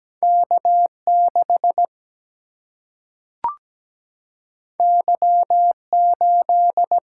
Morse code, heard as a keyed tone, about 17 wpm, 700 Hz. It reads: K6 Y8
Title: Single-Letter Single-Number 17wpm